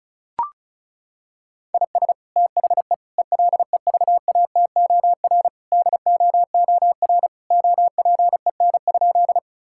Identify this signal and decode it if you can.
Morse code, heard as a keyed tone, about 36 wpm, 700 Hz. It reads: IS THE ELEVATOR DOOR OPEN?
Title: Sentences from Top 1600 Words 35wpm